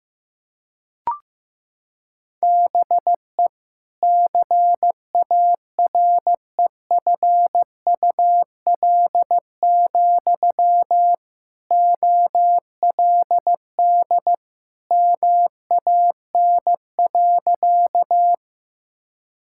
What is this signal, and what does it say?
Morse code, heard as a keyed tone, about 14 wpm, 700 Hz. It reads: BE CAREFUL, OLD MAN.
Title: Sentences from Top 1600 Words 15wpm